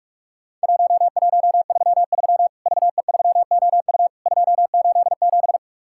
Morse code, heard as a keyed tone, about 41 wpm, 700 Hz. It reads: 1133 VE3OU 287
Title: Worked All Europe DX Contest - QTC Exchange - 45wpm